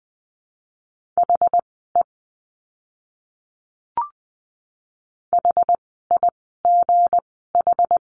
Morse code, heard as a keyed tone, about 20 wpm, spaced as for 14 wpm, 700 Hz. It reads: HE HIGH
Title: 237 Lesson 38f - Letter q - Words 20wpm